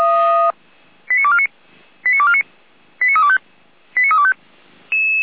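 Five-tone selective call (ZVEI), heard as quick runs of five tones, tones of about 70 ms. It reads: 89249 89249 89325 89325
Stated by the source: Five-tone ZVEI selective call received on an SDR, channel 455.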